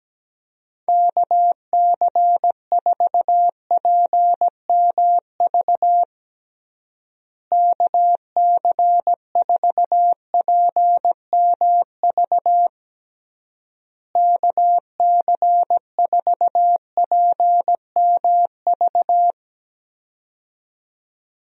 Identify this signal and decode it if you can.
Morse code, heard as a keyed tone, about 17 wpm, 700 Hz. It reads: KC4PMV KC4PMV KC4PMV
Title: USA Call Signs - 6 Characters - Repeated 3x - 17wpm